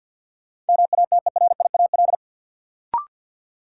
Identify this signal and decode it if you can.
Morse code, heard as a keyed tone, about 51 wpm, 700 Hz. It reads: MATERIAL
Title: Sentences from Top 1100 Words - New Word Drill Alt - 50wpm